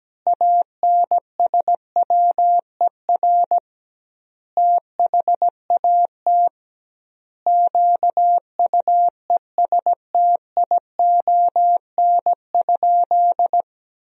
Morse code, heard as a keyed tone, about 17 wpm, 700 Hz. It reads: ANSWER THAT QUESTION?
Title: 2x Word-Spacing - Sentences from Top 400 Words - 17wpm